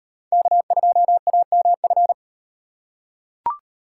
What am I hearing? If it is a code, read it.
Morse code, heard as a keyed tone, about 38 wpm, 700 Hz. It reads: K2AMF